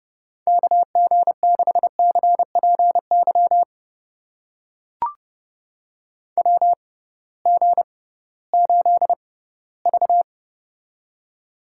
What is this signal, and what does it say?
Morse code, heard as a keyed tone, about 29 wpm, spaced as for 17 wpm, 700 Hz. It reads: KG6CPY WG8V
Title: USA Call Signs - 6x Character Spacing - 30wpm